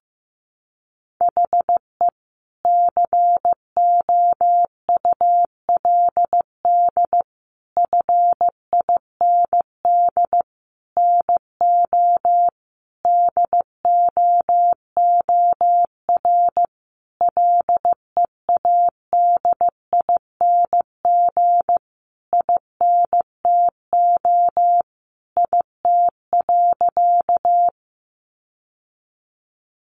Morse code, heard as a keyed tone, about 15 wpm, 700 Hz. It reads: HE COULD FIND NO DOOR LEADING INTO IT.
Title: Sentences from Top 1100 Words 15wpm